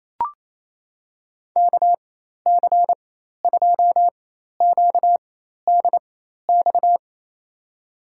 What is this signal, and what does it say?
Morse code, heard as a keyed tone, about 28 wpm, spaced as for 13 wpm, 700 Hz. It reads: KC2QDX
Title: USA Call Signs - 6 Characters - 4x Character Spacing - 28wpm